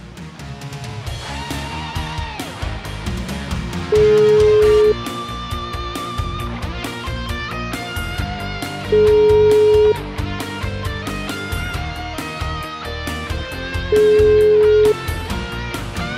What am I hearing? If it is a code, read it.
Ringback tone (Europe, 425 Hz).